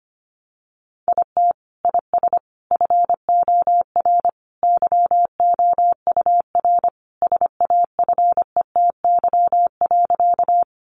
Morse code, heard as a keyed tone, about 25 wpm, 700 Hz. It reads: IT IS FOR YOUR SAFETY.
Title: Sentences from Top 700 Words 25wpm